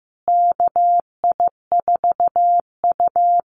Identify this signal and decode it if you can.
Morse code, heard as a keyed tone, about 15 wpm, 700 Hz. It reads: KI4U
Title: USA Call Signs Repeated 3x 15wpm